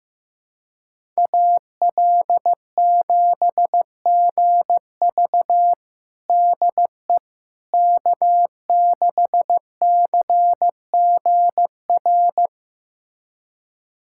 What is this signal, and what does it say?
Morse code, heard as a keyed tone, about 15 wpm, 700 Hz. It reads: AL7GV DE K6CGR
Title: USA Callsign DE Callsign - 5 Characters - 15wpm